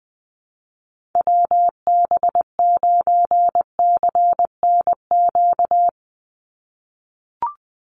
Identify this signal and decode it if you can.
Morse code, heard as a keyed tone, about 20 wpm, 700 Hz. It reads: WB9CNQ
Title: USA Call Signs 20wpm